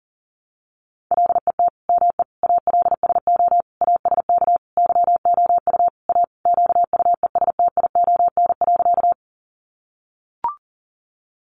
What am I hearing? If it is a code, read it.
Morse code, heard as a keyed tone, about 40 wpm, 700 Hz. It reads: LET ME ALSO ASK YOU A QUESTION.
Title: Sentences from Top 400 Words 40wpm